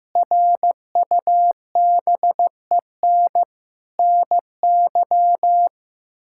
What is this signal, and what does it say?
Morse code, heard as a keyed tone, about 15 wpm, 700 Hz. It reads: RUBEN NY
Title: K1USN SST QSO Practice - 15wpm